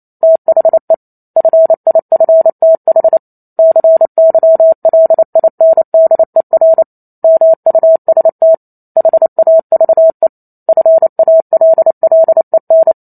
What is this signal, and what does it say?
Morse code, heard as a keyed tone, about 28 wpm, 650 Hz. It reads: THE FIFTH CYLINDER MUST HAVE FALLEN